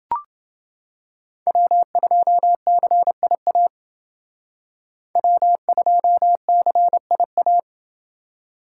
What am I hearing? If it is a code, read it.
Morse code, heard as a keyed tone, about 29 wpm, 700 Hz. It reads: W2CIA W2CIA